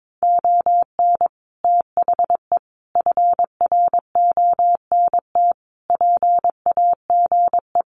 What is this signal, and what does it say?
Morse code, heard as a keyed tone, about 22 wpm, 700 Hz. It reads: ON THE FRONT PAGE
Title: Sets of 5 Words Encore 22wpm